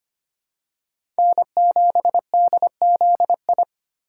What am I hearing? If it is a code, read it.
Morse code, heard as a keyed tone, about 25 wpm, 700 Hz. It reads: N7DZI
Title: USA Call Signs - 5 Characters - 3x Character Spacing - 25wpm